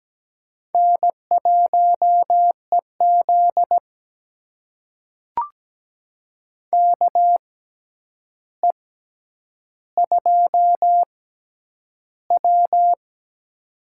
Morse code, heard as a keyed tone, about 17 wpm, spaced as for 9 wpm, 700 Hz. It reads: N1EZ KE2W
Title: USA Call Signs - 4 Characters - 6x Character Spacing - 17wpm